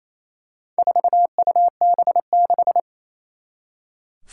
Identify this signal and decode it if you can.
Morse code, heard as a keyed tone, about 27 wpm, 700 Hz. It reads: 4UB6
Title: Four Challenging Letter-Number Combinations 28wpm